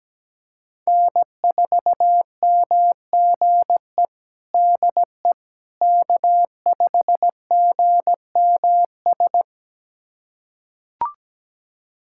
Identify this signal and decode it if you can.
Morse code, heard as a keyed tone, about 17 wpm, 700 Hz. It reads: N4MGE DE K5GMS
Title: USA Callsign DE Callsign - 5 Characters - 17wpm